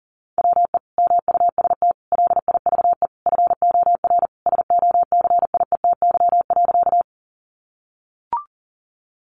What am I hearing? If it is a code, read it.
Morse code, heard as a keyed tone, about 40 wpm, 700 Hz. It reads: WE MUST LIVE FOR SOCIETY.
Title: Sentences from Top 700 Words 40wpm